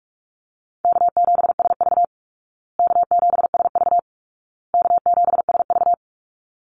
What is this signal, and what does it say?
Morse code, heard as a keyed tone, about 44 wpm, 700 Hz. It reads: K7SV K7SV K7SV
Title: Easy Call Signs Repeated 3x - 45wpm